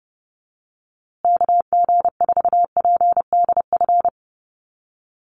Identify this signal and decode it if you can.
Morse code, heard as a keyed tone, about 28 wpm, 700 Hz. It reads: KG4PDF
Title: USA Call Signs Repeated 3x 30wpm